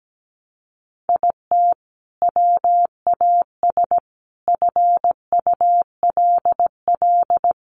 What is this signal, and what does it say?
Morse code, heard as a keyed tone, about 17 wpm, 700 Hz. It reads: IT WAS FULL
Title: Sentences from Top 500 Words 17wpm